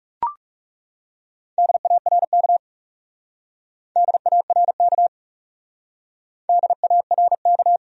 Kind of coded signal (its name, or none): Morse code